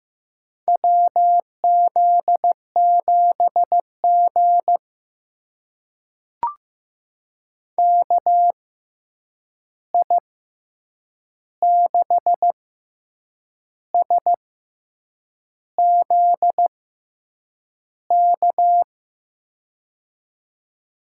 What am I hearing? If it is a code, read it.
Morse code, heard as a keyed tone, about 15 wpm, spaced as for 7 wpm, 700 Hz. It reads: WZ7G KI6SZK
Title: USA Call Signs - 6x Character Spacing - 15wpm